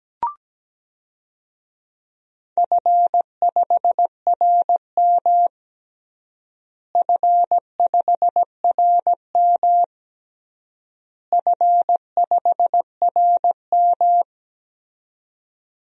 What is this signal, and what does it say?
Morse code, heard as a keyed tone, about 17 wpm, 700 Hz. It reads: F5RM F5RM F5RM
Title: DXCC-100 Call Signs - 4 Characters - Repeated 3x - 17wpm